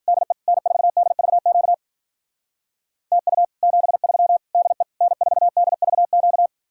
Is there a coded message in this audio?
Yes: Morse code from 0.1 to 6.5 s, heard as a keyed tone, about 51 wpm, 700 Hz. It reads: DE N4DVQ TU 73 DE N4DVQ